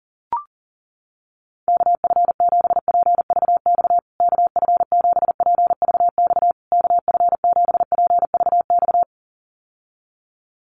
Morse code, heard as a keyed tone, about 40 wpm, 700 Hz. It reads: KF7PVX KF7PVX KF7PVX
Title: USA Call Signs Repeated 3x 40wpm